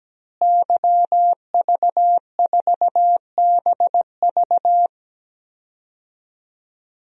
Morse code, heard as a keyed tone, about 17 wpm, 700 Hz. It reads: YV4BV
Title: DXCC-100 Call Signs - 5 Characters - Repeated 3x - 17wpm